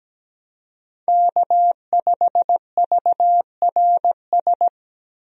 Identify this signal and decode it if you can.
Morse code, heard as a keyed tone, about 17 wpm, 700 Hz. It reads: K5VRS
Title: USA Call Signs - 5 Characters - 17wpm